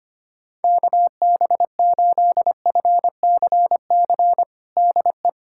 Morse code, heard as a keyed tone, about 25 wpm, 700 Hz. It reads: KB8FCC DE
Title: USA Callsign DE Callsign 25wpm